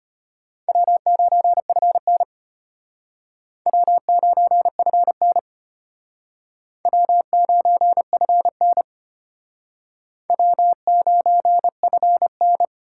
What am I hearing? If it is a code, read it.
Morse code, heard as a keyed tone, about 32 wpm, 700 Hz. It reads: W9FN W9FN W9FN W9FN